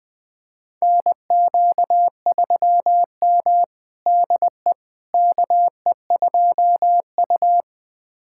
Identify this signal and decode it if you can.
Morse code, heard as a keyed tone, about 20 wpm, 700 Hz. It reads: NQ3M DE KE2U
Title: USA Callsign DE Callsign - 4 Characters - 20wpm